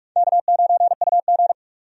Morse code, heard as a keyed tone, about 46 wpm, 700 Hz. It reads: K9UG
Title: USA Call Signs - 4 Characters - 45wpm